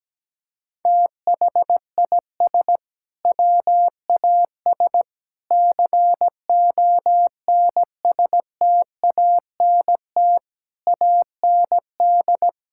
Morse code, heard as a keyed tone, about 17 wpm, 700 Hz. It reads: THIS WAS CONSTANT AND
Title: Sentences from Top 1600 Words 17wpm